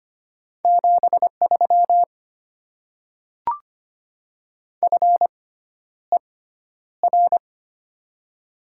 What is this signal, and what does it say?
Morse code, heard as a keyed tone, about 25 wpm, spaced as for 11 wpm, 700 Hz. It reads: 73 FER
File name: Top 20 QSO Elements - 6x Character Spacing - 25wpm